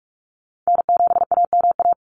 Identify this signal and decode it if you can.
Morse code, heard as a keyed tone, about 44 wpm, 690 Hz. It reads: N7AMA
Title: NA SOTA Chaser Callsigns 45wpm